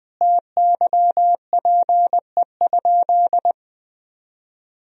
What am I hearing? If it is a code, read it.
Morse code, heard as a keyed tone, about 20 wpm, 700 Hz. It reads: TYPE?
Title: Sentences from Top 1000 Words - Review Part 3 - 20wpm